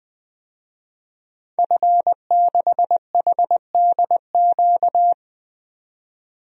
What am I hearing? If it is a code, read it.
Morse code, heard as a keyed tone, about 20 wpm, 700 Hz. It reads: F6HDQ
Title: DXCC-100 Call Signs - 5 Characters - 20wpm